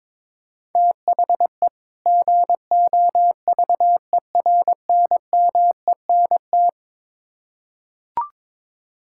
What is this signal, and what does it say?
Morse code, heard as a keyed tone, about 22 wpm, 700 Hz. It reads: THE GOVERNMENT
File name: Build A Phrase - 2-Word Phrases - 22wpm